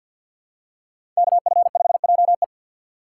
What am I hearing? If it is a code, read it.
Morse code, heard as a keyed tone, about 47 wpm, 700 Hz. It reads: KF5JE
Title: USA Call Signs - 5 Characters - 6x Character Spacing - 50wpm